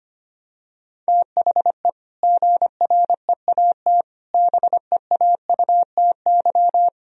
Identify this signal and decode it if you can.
Morse code, heard as a keyed tone, about 25 wpm, 700 Hz. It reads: THE GREAT BEAUTY
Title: Sentences from Top 1000 Words - Review Part 2 - 25wpm